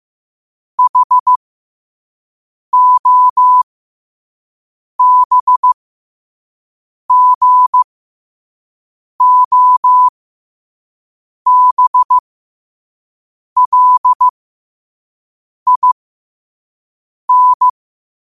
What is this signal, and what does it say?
Morse code, heard as a keyed tone, about 15 wpm, spaced as for 5 wpm, 1000 Hz. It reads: HOBGOBLIN
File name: qotdfiles / 5wpm